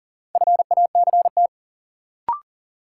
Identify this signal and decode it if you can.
Morse code, heard as a keyed tone, about 40 wpm, 700 Hz. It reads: FACT
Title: Sets of 2 Words - New Word Drill - 40wpm